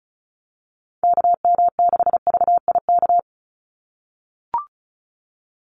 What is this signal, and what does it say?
Morse code, heard as a keyed tone, about 34 wpm, 700 Hz. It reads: KM6VIK
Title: USA Call Signs 35wpm